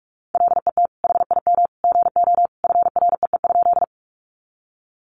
Morse code, heard as a keyed tone, about 44 wpm, 700 Hz. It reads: LET HIM GO FREE?
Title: Sentences from Top 500 Words 45wpm